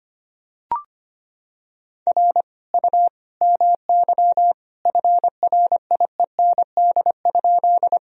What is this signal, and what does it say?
Morse code, heard as a keyed tone, about 25 wpm, 700 Hz. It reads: R U MY FRIEND?
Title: Sentences from Top 300 Words and Abbreviations 25wpm